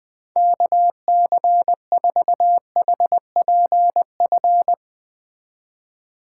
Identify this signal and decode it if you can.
Morse code, heard as a keyed tone, about 20 wpm, 700 Hz. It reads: KC4HPF
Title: Speed Racing - USA Call Signs 17wpm